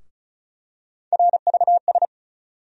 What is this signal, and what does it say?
Morse code, heard as a keyed tone, about 34 wpm, 700 Hz. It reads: RVS
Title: USA Call Sign Postfixes - 3 Letters 35wpm